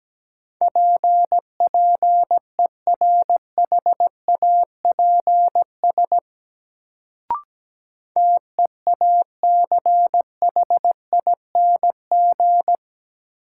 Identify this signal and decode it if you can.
Morse code, heard as a keyed tone, about 17 wpm, 700 Hz. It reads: PPERHAPS TEACHING
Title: Sentences from Top 1100 Words - New Word Drill Alt - 17wpm